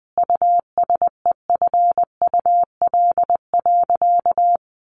Morse code, heard as a keyed tone, about 19 wpm, 700 Hz. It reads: USEFUL.
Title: Sentences from Top 1500 Words 20wpm